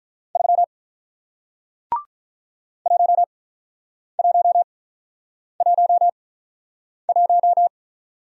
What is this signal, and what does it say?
Morse code, heard as a keyed tone, about 47 wpm, spaced as for 9 wpm, 700 Hz. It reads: 3 1111